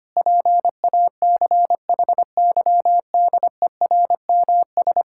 Morse code, heard as a keyed tone, about 25 wpm, 700 Hz. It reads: PACHYDERMS
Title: Pangram Challenge v2 - 25wpm